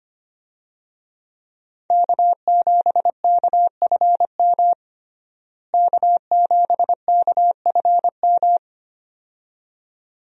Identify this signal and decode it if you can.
Morse code, heard as a keyed tone, about 25 wpm, 700 Hz. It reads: K7KFM K7KFM
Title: USA Call Signs - 5 Characters - Repeated 2x - 25wpm